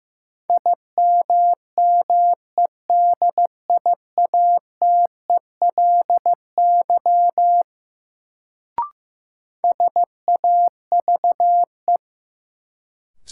Morse code, heard as a keyed tone, about 15 wpm, 700 Hz. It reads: IMMEDIATELY SAVE
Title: Sentences from Top 1100 Words - New Word Drill - 15wpm